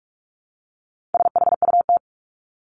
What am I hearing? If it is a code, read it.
Morse code, heard as a keyed tone, about 44 wpm, 700 Hz. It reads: SHUT
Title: Top 1751-2000 Words 45wpm